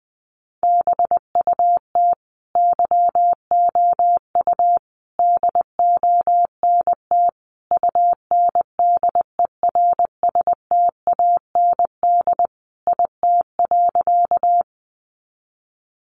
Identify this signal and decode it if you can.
Morse code, heard as a keyed tone, about 20 wpm, 700 Hz. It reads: BUT YOU DONT UNDERSTAND IT.
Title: Sentences from Top 600 Words 20wpm